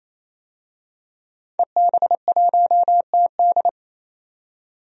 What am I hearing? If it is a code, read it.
Morse code, heard as a keyed tone, about 28 wpm, 700 Hz. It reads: EB1TD
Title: DXCC-100 Call Signs - 5 Characters - 28wpm